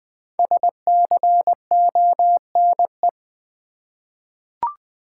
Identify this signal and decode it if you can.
Morse code, heard as a keyed tone, about 20 wpm, 690 Hz. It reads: SCONE
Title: Difficult Words with Hints - 20wpm